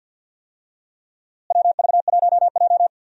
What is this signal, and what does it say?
Morse code, heard as a keyed tone, about 51 wpm, 700 Hz. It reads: WV1J
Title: USA Call Signs - 4 Characters - Repeated 2x - 50wpm